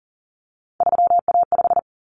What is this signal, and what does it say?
Morse code, heard as a keyed tone, about 40 wpm, 700 Hz. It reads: 3A5
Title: Common Cut Numbers - 3-Digit 40wpm